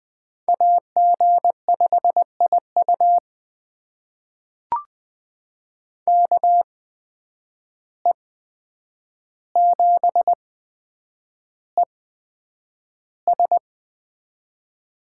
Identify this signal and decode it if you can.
Morse code, heard as a keyed tone, about 20 wpm, spaced as for 9 wpm, 700 Hz. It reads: AG5IU KE7ES
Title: USA Call Signs - 8x Character Spacing - 20wpm